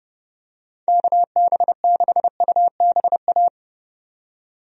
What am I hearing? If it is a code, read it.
Morse code, heard as a keyed tone, about 30 wpm, 700 Hz. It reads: KB6UBA
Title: USA Call Signs - 6 Characters - 30wpm